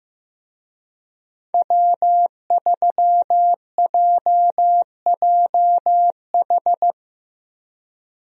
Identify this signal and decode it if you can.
Morse code, heard as a keyed tone, about 15 wpm, 700 Hz. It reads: W3JJH
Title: USA Call Signs - 5 Characters - Repeated 2x - 15wpm